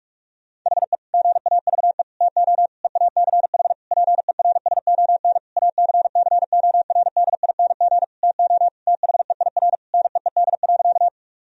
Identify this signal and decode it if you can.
Morse code, heard as a keyed tone, about 46 wpm, 700 Hz. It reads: HE GAVE TO EACH PERSON ACCORDING TO THEIR NEED.
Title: Sentences from Top 1000 Words - Review Part 4 - 45wpm